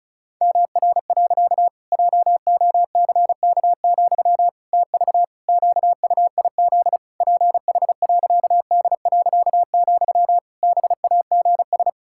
Morse code, heard as a keyed tone, about 36 wpm, 700 Hz. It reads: MR. JOCK, TV QUIZ PH.D., BAGS
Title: Pangram Challenge v2 - 35wpm